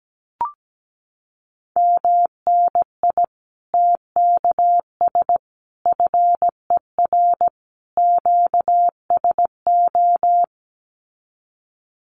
Morse code, heard as a keyed tone, about 17 wpm, 700 Hz. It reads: MNI TKS FER QSO
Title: Common QSO Phrases 17wpm